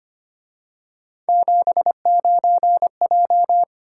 Morse code, heard as a keyed tone, about 29 wpm, 700 Hz. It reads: 79J
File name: Speed-Racing - Three Letter-Number Combinations - 25wpm